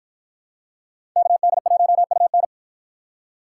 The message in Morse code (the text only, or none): KD1UN